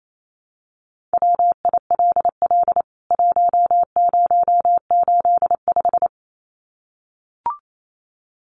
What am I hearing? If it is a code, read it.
Morse code, heard as a keyed tone, about 28 wpm, 700 Hz. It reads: WILL 1085
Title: CWT Exchange 28wpm